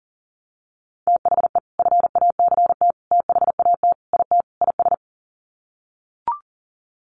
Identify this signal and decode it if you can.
Morse code, heard as a keyed tone, about 40 wpm, 700 Hz. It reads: THE FACT THAT IT IS
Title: Sets of 5 Words Encore2 40wpm